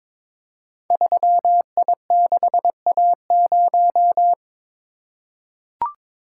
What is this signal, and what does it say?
Morse code, heard as a keyed tone, about 22 wpm, 700 Hz. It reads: 3I6A0
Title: Code Groups - 5 Letter:Numbers - 22wpm